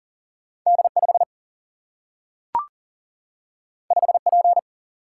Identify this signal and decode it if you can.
Morse code, heard as a keyed tone, about 40 wpm, 700 Hz. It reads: D 5 5 P
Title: Single-Letter Single-Number 40wpm